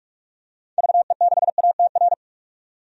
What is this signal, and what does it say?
Morse code, heard as a keyed tone, about 46 wpm, 700 Hz. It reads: VE6ATR
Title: DXCC-100 Call Signs - 6 Characters - 45wpm